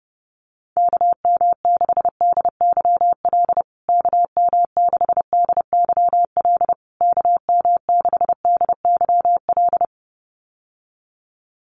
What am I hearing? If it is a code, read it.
Morse code, heard as a keyed tone, about 30 wpm, 700 Hz. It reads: KM6DYL KM6DYL KM6DYL
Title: USA Call Signs Repeated 3x 30wpm